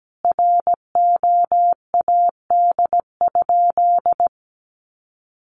Morse code, heard as a keyed tone, about 17 wpm, 700 Hz. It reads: ROAD?
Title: Sentences from Top 1400 Words 17wpm